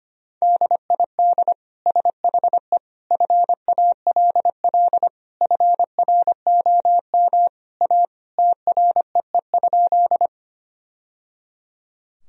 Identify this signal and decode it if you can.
Morse code, heard as a keyed tone, about 25 wpm, 700 Hz. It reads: DID SHE FALL FROM A TREE?
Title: Sentences from Top 1000 Words - Review Part 2 - 25wpm